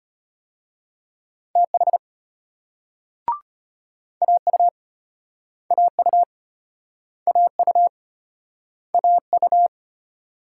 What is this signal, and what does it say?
Morse code, heard as a keyed tone, about 36 wpm, 700 Hz. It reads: TH AU AU AU AU